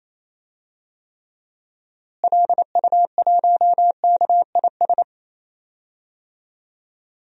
Morse code, heard as a keyed tone, about 28 wpm, 700 Hz. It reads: LU1KIS
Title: DXCC-100 Call Signs - 6 Characters - Repeated 2x - 28wpm